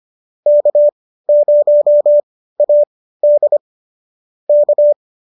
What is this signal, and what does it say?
Morse code, heard as a keyed tone, about 25 wpm, spaced as for 15 wpm, 590 Hz. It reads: K0AD K